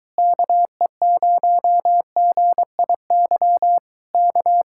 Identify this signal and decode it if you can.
Morse code, heard as a keyed tone, about 23 wpm, 700 Hz. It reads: KE0GIY K